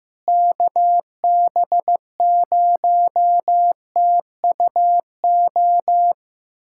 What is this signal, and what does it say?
Morse code, heard as a keyed tone, about 15 wpm, 700 Hz. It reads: KB0TUO
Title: USA Callsign DE Callsign 15wpm